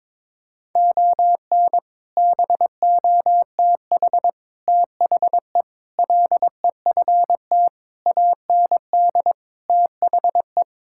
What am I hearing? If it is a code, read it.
Morse code, heard as a keyed tone, about 22 wpm, 700 Hz. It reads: ON BOTH THE LEFT AND THE
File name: practice / 22wpm / Sets of 7 Words - 22wpm